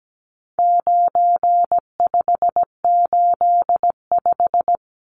Morse code, heard as a keyed tone, about 17 wpm, 700 Hz. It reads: 9585
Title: Numbers - 4-Digit 17wpm